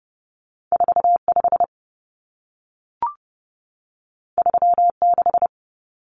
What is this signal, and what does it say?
Morse code, heard as a keyed tone, about 30 wpm, 700 Hz. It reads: 45 36
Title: Common Cut Numbers - 2-Digit 30wpm